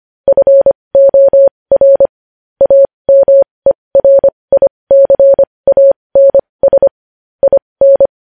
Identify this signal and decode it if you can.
Morse code, heard as a keyed tone, about 25 wpm, spaced as for 20 wpm, 550 Hz. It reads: FOR AMERICANS IN
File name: podcast / News headlines-1x-25x20